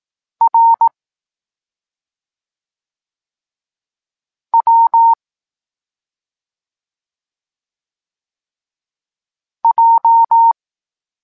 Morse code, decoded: R W J